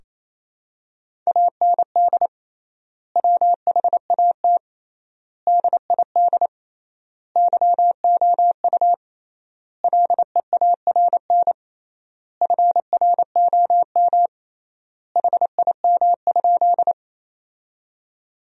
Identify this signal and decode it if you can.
Morse code, heard as a keyed tone, about 27 wpm, 700 Hz. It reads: AND WHAT DID YOU LEARN FROM HIM?
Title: 3x Word-Spacing - Sentences from Top 300 Words - 28wpm